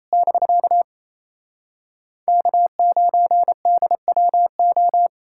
Morse code, heard as a keyed tone, about 37 wpm, 700 Hz. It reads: <BK> K9DWO